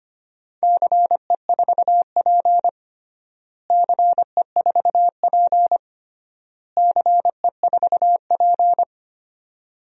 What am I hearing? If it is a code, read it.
Morse code, heard as a keyed tone, about 25 wpm, 700 Hz. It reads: CE4P CE4P CE4P
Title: DXCC-100 Call Signs - 4 Characters - Repeated 3x - 25wpm